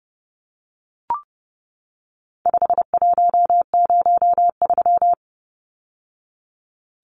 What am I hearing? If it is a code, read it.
Morse code, heard as a keyed tone, about 30 wpm, 700 Hz. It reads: 5103
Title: Numbers - 4-Digit 30wpm